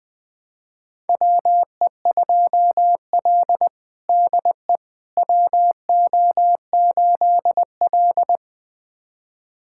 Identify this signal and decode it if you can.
Morse code, heard as a keyed tone, about 21 wpm, 700 Hz. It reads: WE2L DE WO8L
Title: USA Callsign DE Callsign - 4 Characters - 20wpm